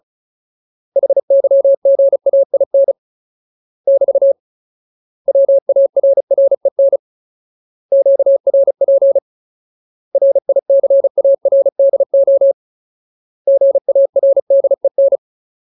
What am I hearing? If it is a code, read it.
Morse code, heard as a keyed tone, about 34 wpm, 550 Hz. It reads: HYGAIN <BT> WARREN QRP RICARDO GARDEN
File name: Ultimate Word List / 35WPM Ultimate